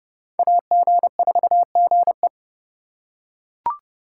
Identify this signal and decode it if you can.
Morse code, heard as a keyed tone, about 28 wpm, 700 Hz. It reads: AG4GE